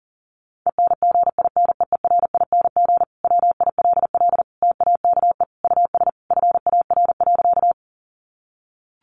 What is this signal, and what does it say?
Morse code, heard as a keyed tone, about 40 wpm, 700 Hz. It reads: ENGINEERING WILL TAKE US FAR.
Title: Sentences from Top 1500 Words 40wpm